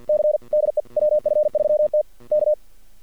Morse code, heard as a keyed tone, about 48 wpm, 600 Hz. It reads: C DE WF3T K
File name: archived / CRSnet